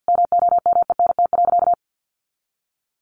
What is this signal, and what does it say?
Morse code, heard as a keyed tone, about 48 wpm, 700 Hz. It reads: MOMENT.
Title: Sentences from Top 1000 Words 50wpm